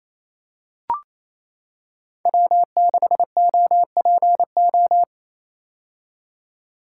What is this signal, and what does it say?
Morse code, heard as a keyed tone, about 28 wpm, 700 Hz. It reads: W6OPO
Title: K1USN SST QSO Practice - 28wpm